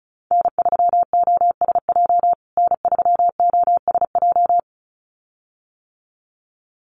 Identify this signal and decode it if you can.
Morse code, heard as a keyed tone, about 34 wpm, 700 Hz. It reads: N3OSJ N3OSJ
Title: USA Call Signs Repeated 2x 35wpm